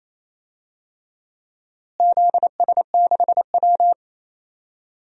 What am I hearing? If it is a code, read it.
Morse code, heard as a keyed tone, about 28 wpm, 700 Hz. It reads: ZS6W
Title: DXCC-100 Call Signs - 4 Characters - Repeated 3x - 28wpm